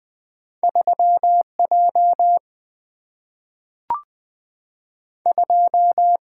Morse code, heard as a keyed tone, about 20 wpm, 700 Hz. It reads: 3 J 2
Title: Two Letter-Number Combinations - 20wpm